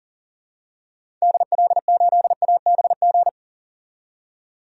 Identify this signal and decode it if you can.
Morse code, heard as a keyed tone, about 40 wpm, 700 Hz. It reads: DL8ABG
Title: DXCC-100 Call Signs - 6 Characters - 40wpm